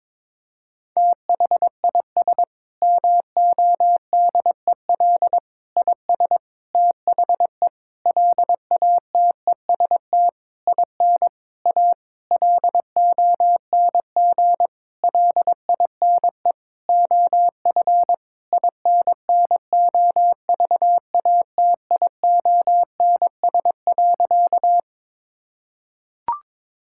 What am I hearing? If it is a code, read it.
Morse code, heard as a keyed tone, about 22 wpm, 700 Hz. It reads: THIS MODEL IS THE LATEST IN A LONG LINE OF INNOVATIONS.
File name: Sentences with 7-Word Phrases - 22wpm